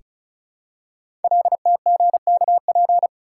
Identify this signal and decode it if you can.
Morse code, heard as a keyed tone, about 36 wpm, 700 Hz. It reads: LTGKP
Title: Code Groups - 5 Letter:Numbers - 35wpm